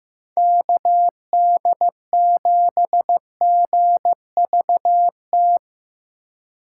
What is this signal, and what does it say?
Morse code, heard as a keyed tone, about 15 wpm, 700 Hz. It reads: KD7GVT
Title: USA Call Signs - 4x Character Spacing - 15wpm